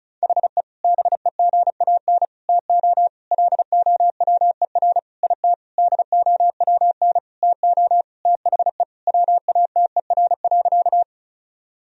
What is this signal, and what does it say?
Morse code, heard as a keyed tone, about 34 wpm, 700 Hz. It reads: HE BEGAN TO LOWER IT DOWN TO THE WATER.
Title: Sentences from Top 1000 Words - Review Part 4 - 35wpm